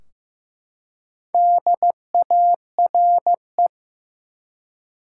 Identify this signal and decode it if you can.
Morse code, heard as a keyed tone, about 16 wpm, 700 Hz. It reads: DARE
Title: Four Letter Words - 15wpm